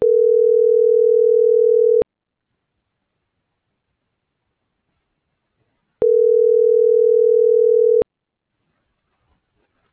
Ringback tone (North America), 440 + 480 Hz.